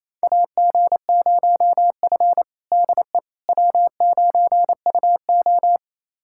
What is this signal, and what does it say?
Morse code, heard as a keyed tone, about 28 wpm, 700 Hz. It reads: AG0F DE W9UO
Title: USA Callsign DE Callsign - 4 Characters - 28wpm